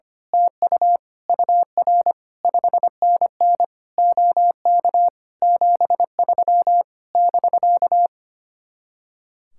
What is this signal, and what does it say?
Morse code, heard as a keyed tone, about 24 wpm, 700 Hz. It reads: TU UR 5NN OK 73 <BK>
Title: Speed-Racing Common QSO Phrases Encore - 25wpm